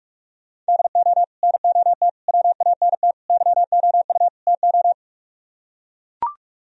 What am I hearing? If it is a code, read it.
Morse code, heard as a keyed tone, about 44 wpm, 700 Hz. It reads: DO NOT WANT YOU TO